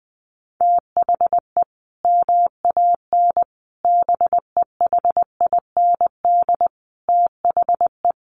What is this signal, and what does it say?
Morse code, heard as a keyed tone, about 20 wpm, 700 Hz. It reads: THE MAN BEHIND THE
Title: A Sound Of Thunder 20wpm